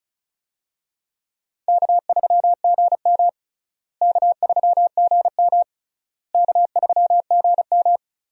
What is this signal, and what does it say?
Morse code, heard as a keyed tone, about 36 wpm, 700 Hz. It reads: K3GM K3GM K3GM
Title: USA Call Signs - 4 Characters - Repeated 3x - 35wpm